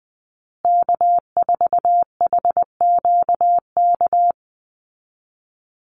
Morse code, heard as a keyed tone, about 20 wpm, 700 Hz. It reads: K4HQK
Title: Call Signs Repeated 2x - 20wpm